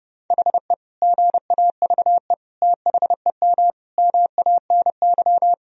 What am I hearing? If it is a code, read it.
Morse code, heard as a keyed tone, about 28 wpm, 700 Hz. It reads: HE GAVE THEM MANY
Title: Sentences from Top 1000 Words - Review Part 3 - 30wpm